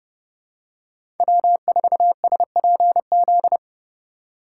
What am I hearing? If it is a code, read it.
Morse code, heard as a keyed tone, about 30 wpm, 700 Hz. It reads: W4SPZ
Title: USA Call Signs - 4x Character Spacing - 30wpm